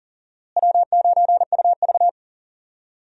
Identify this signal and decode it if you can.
Morse code, heard as a keyed tone, about 40 wpm, 700 Hz. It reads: W9UV
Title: USA Call Signs - 4 Characters - Repeated 2x - 40wpm